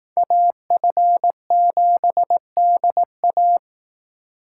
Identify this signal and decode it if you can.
Morse code, heard as a keyed tone, about 18 wpm, 700 Hz. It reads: AF7DA
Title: Speed Racing - USA Call Signs 15wpm